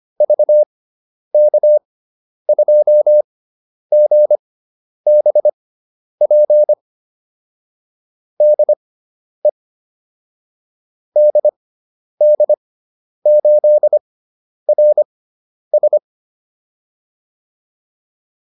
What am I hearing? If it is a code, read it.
Morse code, heard as a keyed tone, about 25 wpm, spaced as for 10 wpm, 600 Hz. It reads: VK2GBP DE DD8RS
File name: QSO405 10